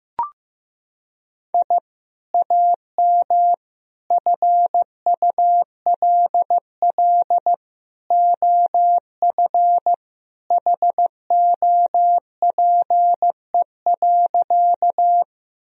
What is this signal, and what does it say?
Morse code, heard as a keyed tone, about 15 wpm, 700 Hz. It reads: I AM FULL OF HOPE.
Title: Sentences from Top 1000 Words - Review Part 3 - 15wpm